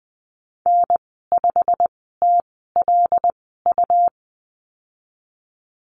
Morse code, heard as a keyed tone, about 20 wpm, spaced as for 15 wpm, 700 Hz. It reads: N5TLU
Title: 128 Lesson 21g - Number 6 - Call Signs 20wpm